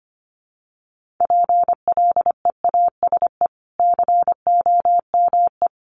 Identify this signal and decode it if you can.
Morse code, heard as a keyed tone, about 25 wpm, 700 Hz. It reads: PLEASE COME
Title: Sentences from Top 1500 Words 25wpm